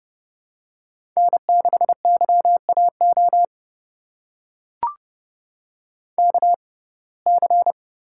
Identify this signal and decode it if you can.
Morse code, heard as a keyed tone, about 30 wpm, spaced as for 22 wpm, 700 Hz. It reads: N6YAO KC
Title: USA Call Signs - 5 Characters - 6x Character Spacing - 30wpm